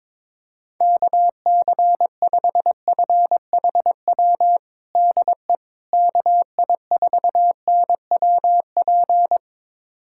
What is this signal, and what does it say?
Morse code, heard as a keyed tone, about 22 wpm, 700 Hz. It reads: KC5FHW DE KI4NWP
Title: USA Callsign DE Callsign 22wpm